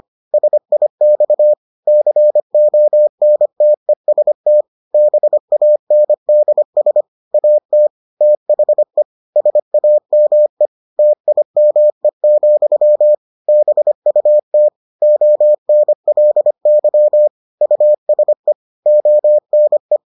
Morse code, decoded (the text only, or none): SIX CONTEST BANDS AT THE SAME TIME, BUT ONLY USE ONE